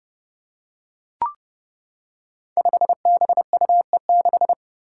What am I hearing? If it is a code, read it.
Morse code, heard as a keyed tone, about 30 wpm, 700 Hz. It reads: 5BUE6
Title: Five Challenging Letter-Number Combinations 30wpm